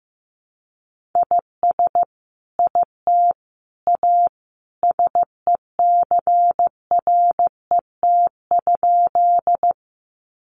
Morse code, heard as a keyed tone, about 15 wpm, 700 Hz. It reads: IS IT A SECRET?
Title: Sentences from Top 1200 Words 15wpm